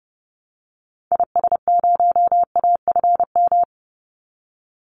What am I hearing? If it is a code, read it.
Morse code, heard as a keyed tone, about 30 wpm, 700 Hz. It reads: IS0AFM
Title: Call Signs - 30wpm